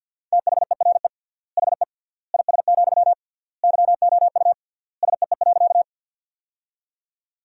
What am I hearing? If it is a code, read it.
Morse code, heard as a keyed tone, about 50 wpm, 700 Hz. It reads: THERE HE IS, YOU SEE.
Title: 3x Word-Spacing - Sentences from Top 100 Words - 50wpm